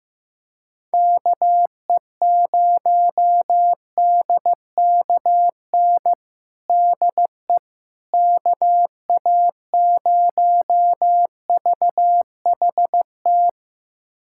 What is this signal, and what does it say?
Morse code, heard as a keyed tone, about 15 wpm, 700 Hz. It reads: KE0DKN DE KA0VHT
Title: USA Callsign DE Callsign - 6 Characters - 15wpm